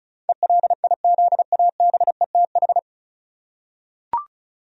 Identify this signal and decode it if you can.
Morse code, heard as a keyed tone, about 36 wpm, 700 Hz. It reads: ELIZABETH
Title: USA Common Names - 35wpm